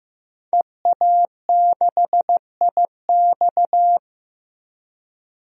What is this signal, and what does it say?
Morse code, heard as a keyed tone, about 15 wpm, 700 Hz. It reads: EA6IX
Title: DXCC-100 Call Signs - 5 Characters - Repeated 3x - 15wpm